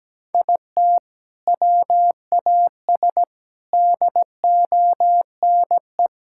Morse code, heard as a keyed tone, about 17 wpm, 700 Hz. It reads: IT WAS DONE
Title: Sentences from Top 1000 Words - Review Part 5 - 17wpm